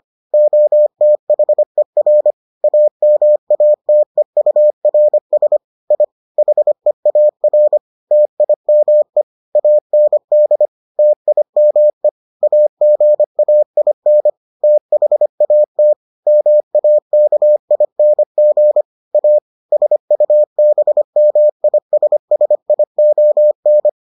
Morse code, heard as a keyed tone, about 25 wpm, 600 Hz. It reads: OTHER AMATEURS I HEAR TIME AND TIME AGAIN THAT MAKING A SUBMISSION